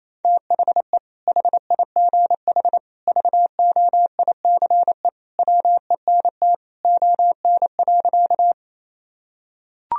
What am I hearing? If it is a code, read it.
Morse code, heard as a keyed tone, about 28 wpm, 700 Hz. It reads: THE HIGH VOICE WENT ON.
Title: Sentences from Top 1000 Words - Review Part 1 - 28wpm